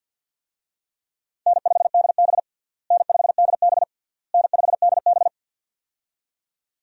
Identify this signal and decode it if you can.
Morse code, heard as a keyed tone, about 51 wpm, 700 Hz. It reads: N5DB N5DB N5DB
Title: USA Call Signs - 4 Characters - Repeated 3x - 50wpm